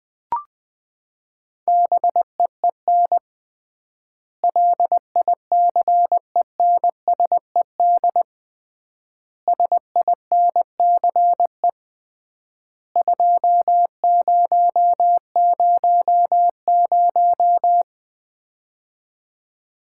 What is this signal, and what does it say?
Morse code, heard as a keyed tone, about 20 wpm, 700 Hz. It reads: BEEN LICENSED SINCE 2000
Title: Common QSO Phrases - 3x Word-Spacing - 20wpm